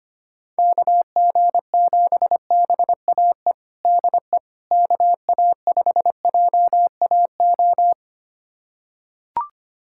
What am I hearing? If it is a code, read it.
Morse code, heard as a keyed tone, about 25 wpm, 700 Hz. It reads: KG7BAE DE KA5JAO
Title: USA Callsign DE Callsign - 6 Characters - 25wpm